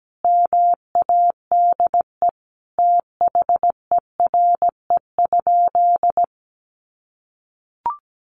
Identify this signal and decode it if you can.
Morse code, heard as a keyed tone, about 17 wpm, 700 Hz. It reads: MADE THERE?
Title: Sentences from Top 1400 Words 17wpm